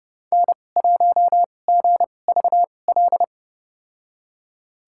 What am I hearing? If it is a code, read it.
Morse code, heard as a keyed tone, about 30 wpm, spaced as for 22 wpm, 700 Hz. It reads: N1GVL
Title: USA Call Signs - 5 Characters - 2x Character Spacing - 30wpm